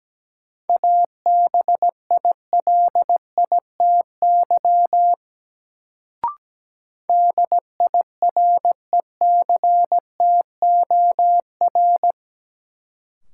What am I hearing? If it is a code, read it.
Morse code, heard as a keyed tone, about 17 wpm, 700 Hz. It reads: ABILITY DIRECTOR
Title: Sets of 3 Words - New Word Drill - 17wpm